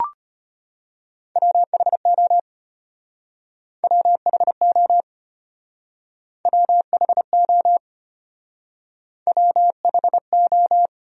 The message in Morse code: WHO WHO WHO WHO

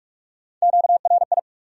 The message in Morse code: QRI

